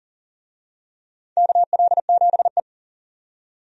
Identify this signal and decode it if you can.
Morse code, heard as a keyed tone, about 37 wpm, 700 Hz. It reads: KL7E
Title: DXCC-100 Call Signs - 4 Characters - 40wpm